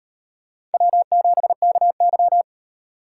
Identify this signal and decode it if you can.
Morse code, heard as a keyed tone, about 38 wpm, 700 Hz. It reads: W7KY